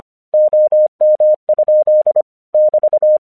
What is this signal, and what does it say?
Morse code, heard as a keyed tone, about 26 wpm, 610 Hz. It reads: OM? <BT>
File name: 1 English 25wpm / AQC30